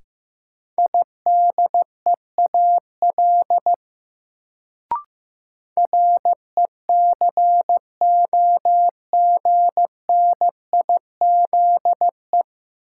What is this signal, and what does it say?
Morse code, heard as a keyed tone, about 16 wpm, 700 Hz. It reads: IDEAL RECOGNIZE
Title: Sentences from Top 1400 Words - New Word Drill Alt - 15wpm